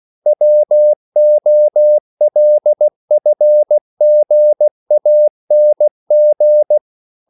Morse code, heard as a keyed tone, about 16 wpm, 600 Hz. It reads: WOLFGANG
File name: CW / Practice / EnglishQuotations0416